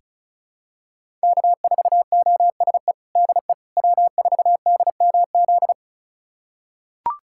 Morse code, heard as a keyed tone, about 36 wpm, 700 Hz. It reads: K4OSE DE W4DMZ
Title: USA Callsign DE Callsign - 5 Characters - 35wpm